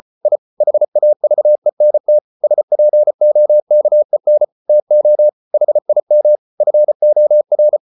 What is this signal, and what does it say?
Morse code, heard as a keyed tone, about 34 wpm, 600 Hz. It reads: I HAVENT SPOKEN TO HIM FOR